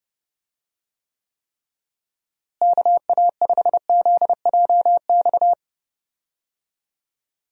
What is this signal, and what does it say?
Morse code, heard as a keyed tone, about 30 wpm, 700 Hz. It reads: KA5ZJX
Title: USA Call Signs - 6 Characters - Repeated 3x - 30wpm